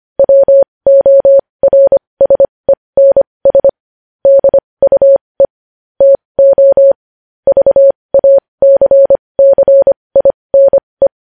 Morse code, heard as a keyed tone, about 25 wpm, spaced as for 20 wpm, 550 Hz. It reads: WORSENS DUE TO VACCINE
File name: podcast / News headlines-1x-25x20